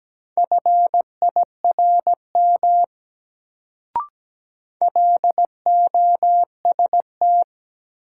Morse code, heard as a keyed tone, about 17 wpm, 700 Hz. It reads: FIRM LOST
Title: Sentences from Top 1400 Words - New Word Drill Alt - 17wpm